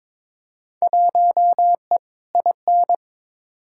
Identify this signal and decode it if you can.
Morse code, heard as a keyed tone, about 22 wpm, 700 Hz. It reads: 1E IN
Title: Field Day Exchanges - 22wpm